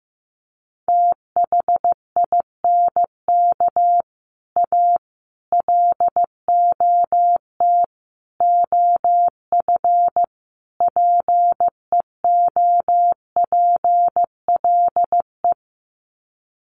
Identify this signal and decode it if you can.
Morse code, heard as a keyed tone, about 15 wpm, 700 Hz. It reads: THINK A LOT OF PEOPLE
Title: Sets of 5 Words 15wpm